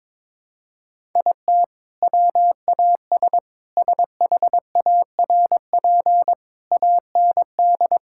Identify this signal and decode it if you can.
Morse code, heard as a keyed tone, about 22 wpm, 700 Hz. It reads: IT WAS SHARP AND
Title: Sentences from Top 1600 Words 22wpm